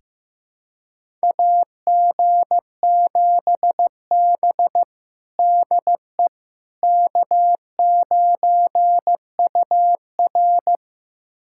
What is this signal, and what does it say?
Morse code, heard as a keyed tone, about 15 wpm, 700 Hz. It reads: AG7B DE K9UR
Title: USA Callsign DE Callsign - 4 Characters - 15wpm